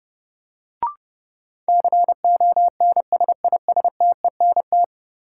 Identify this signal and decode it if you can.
Morse code, heard as a keyed tone, about 32 wpm, 700 Hz. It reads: CONSISTENT
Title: Sentences from Top 1500 Words - New Word Drill - 30wpm